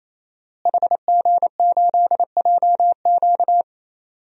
Morse code, heard as a keyed tone, about 28 wpm, 700 Hz. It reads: HG8JQ
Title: DXCC-100 Call Signs - 5 Characters - Repeated 3x - 28wpm